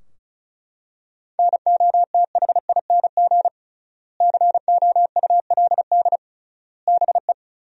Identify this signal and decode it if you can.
Morse code, heard as a keyed tone, about 34 wpm, 700 Hz. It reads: NOTHING COULD BE
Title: 3x Word-Spacing - Sentences from Top 500 Words - 35wpm